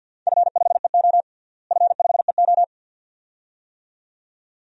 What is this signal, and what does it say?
Morse code, heard as a keyed tone, about 52 wpm, 700 Hz. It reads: F5EO F5EO
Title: DXCC-100 Call Signs - 4 Characters - Repeated 2x - 50wpm